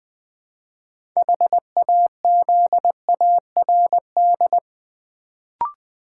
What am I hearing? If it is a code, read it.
Morse code, heard as a keyed tone, about 20 wpm, 690 Hz. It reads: HAZARD
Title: Build A Word - 2 Letters At A Time - 6-Letter Words - 20wpm